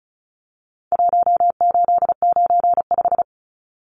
Morse code, heard as a keyed tone, about 34 wpm, 700 Hz. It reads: 1895
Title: Numbers - Contest Style - 4-Digit 35wpm